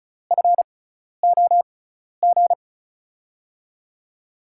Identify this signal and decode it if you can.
Morse code, heard as a keyed tone, about 36 wpm, spaced as for 12 wpm, 690 Hz. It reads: FOG
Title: USA Call Signs - 6 Characters - 6x Character Spacing - 35wpm